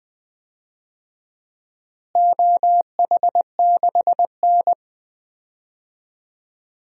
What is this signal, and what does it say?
Morse code, heard as a keyed tone, about 20 wpm, 700 Hz. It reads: OH6N
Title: DXCC-100 Call Signs - 4 Characters - Repeated 3x - 20wpm